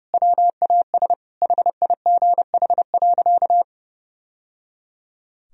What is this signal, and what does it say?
Morse code, heard as a keyed tone, about 30 wpm, 700 Hz. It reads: WAS HIGH.
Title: Sentences from Top 1000 Words - Review Part 4 - 30wpm